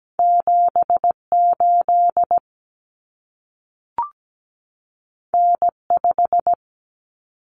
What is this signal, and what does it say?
Morse code, heard as a keyed tone, about 17 wpm, 700 Hz. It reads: 78 N5
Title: Common Cut Numbers - 2-Digit 17wpm